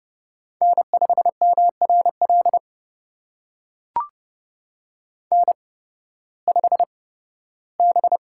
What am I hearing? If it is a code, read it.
Morse code, heard as a keyed tone, about 30 wpm, spaced as for 16 wpm, 700 Hz. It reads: N5MRL N5B
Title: USA Call Signs - 8x Character Spacing - 30wpm